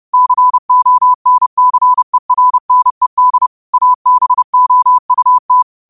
Morse code, heard as a keyed tone, about 30 wpm, 1000 Hz. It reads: CONCERNED ABOUT
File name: qotdfiles / 30wpm